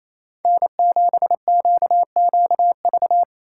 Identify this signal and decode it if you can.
Morse code, heard as a keyed tone, about 28 wpm, 690 Hz. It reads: N7QQV
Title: USA Call Signs - 5 Characters - 4x Character Spacing - 28wpm